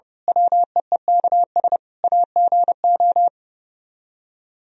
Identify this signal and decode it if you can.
Morse code, heard as a keyed tone, about 29 wpm, 700 Hz. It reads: WEEKS AGO
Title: Speed-Racing - Sets of 5 Words 30wpm